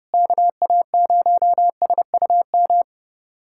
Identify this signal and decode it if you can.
Morse code, heard as a keyed tone, about 32 wpm, 700 Hz. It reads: KA0SUM
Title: USA Callsign DE Callsign - 6 Characters - 30wpm